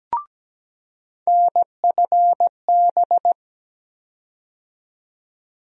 Morse code, heard as a keyed tone, about 17 wpm, 700 Hz. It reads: NFB
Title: USA Call Sign Postfixes - 3 Letters 17wpm